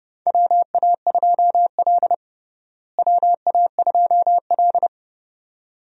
Morse code, heard as a keyed tone, about 30 wpm, 700 Hz. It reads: WA2L WA2L
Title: USA Call Signs - 4 Characters - Repeated 2x - 30wpm